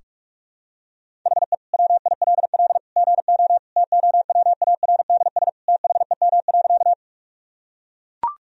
Morse code, heard as a keyed tone, about 41 wpm, 700 Hz. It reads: HE WILL GO TOWARDS THEM.
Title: Sentences from Top 1000 Words - Review Part 2 - 45wpm